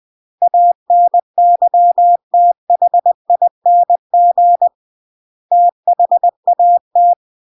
Morse code, heard as a keyed tone, about 20 wpm, 700 Hz. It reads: ANYTHING THAT